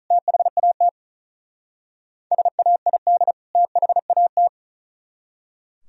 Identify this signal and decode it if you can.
Morse code, heard as a keyed tone, about 43 wpm, 700 Hz. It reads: THAT SAID THAT
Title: Speed-Racing - Sets of 2 Words 35wpm